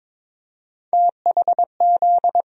Morse code, heard as a keyed tone, about 21 wpm, 700 Hz. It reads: THZ
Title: USA Call Sign Postfixes - 3 Letters 22wpm